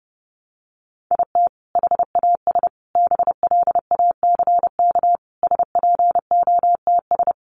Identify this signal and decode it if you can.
Morse code, heard as a keyed tone, about 30 wpm, 700 Hz. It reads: IT HAS BLACK SPOTS
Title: Additional Easy Phrases 30wpm